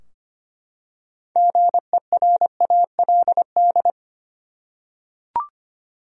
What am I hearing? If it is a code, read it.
Morse code, heard as a keyed tone, about 25 wpm, 700 Hz. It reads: GERALD
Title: USA Common Names - 25wpm